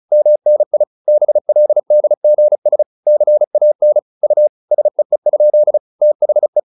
Morse code, heard as a keyed tone, about 34 wpm, 600 Hz. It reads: MNI BLDGS CAN U SEE? THE